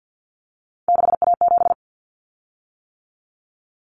Morse code, heard as a keyed tone, about 50 wpm, 700 Hz. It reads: 6A7
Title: Common Cut Numbers - 3-Digit 50wpm